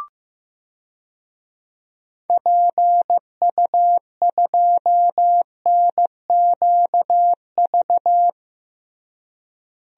Morse code, heard as a keyed tone, about 15 wpm, 700 Hz. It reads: PU2NQV
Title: DXCC-100 Call Signs - 6 Characters - Repeated 3x - 15wpm